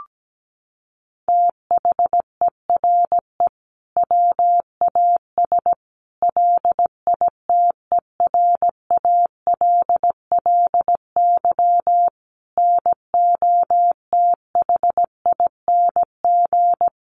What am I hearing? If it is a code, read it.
Morse code, heard as a keyed tone, about 17 wpm, 700 Hz. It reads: THERE WAS LITERALLY NOTHING
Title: Sentences from Top 1400 Words 17wpm